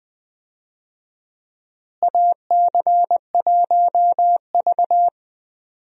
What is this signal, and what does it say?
Morse code, heard as a keyed tone, about 20 wpm, 700 Hz. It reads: AC1V
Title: USA Call Signs - 4 Characters - Repeated 3x - 20wpm